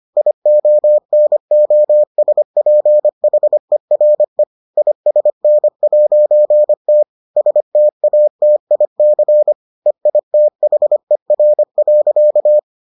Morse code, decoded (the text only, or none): IONOSPHERE ISN'T STATIC EITHER.